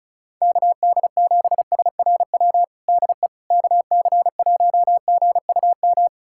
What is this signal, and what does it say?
Morse code, heard as a keyed tone, about 36 wpm, 700 Hz. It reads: KD7SRW DE KC1GUM
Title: USA Callsign DE Callsign - 6 Characters - 35wpm